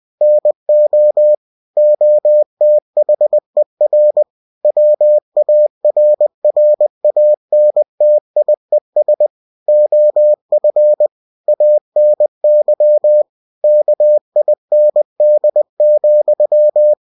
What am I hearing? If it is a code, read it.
Morse code, heard as a keyed tone, about 20 wpm, 600 Hz. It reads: NO OTHER WARRANTIES OF ANY KIND,
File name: cw-books / winnie-the-pooh / winnie-20wpm-page-0119